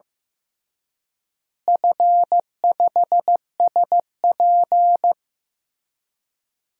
Morse code, heard as a keyed tone, about 15 wpm, 700 Hz. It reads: F5SP
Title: DXCC-100 Call Signs - 4 Characters - Repeated 3x - 15wpm